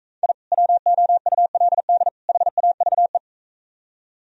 Morse code, heard as a keyed tone, about 43 wpm, 700 Hz. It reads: I WOULD HAVE